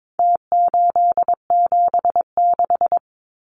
Morse code, heard as a keyed tone, about 22 wpm, 700 Hz. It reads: T876
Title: Numbers - Contest Style - 4-Digit 22wpm